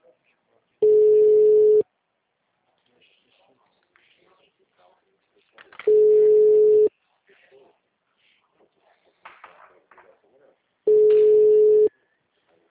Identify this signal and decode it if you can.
Ringback tone (Europe, 425 Hz), with speech.
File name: out- -2014- - - .9362